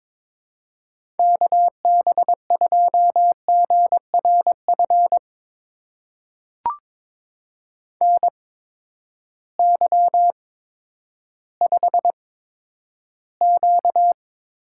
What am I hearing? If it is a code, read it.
Morse code, heard as a keyed tone, about 22 wpm, spaced as for 11 wpm, 700 Hz. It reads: KB2GRF NY5Q
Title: USA Call Signs - 8x Character Spacing - 22wpm